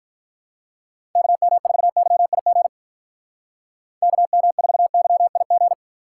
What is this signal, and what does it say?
Morse code, heard as a keyed tone, about 45 wpm, 700 Hz. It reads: KM4YIG KM4YIG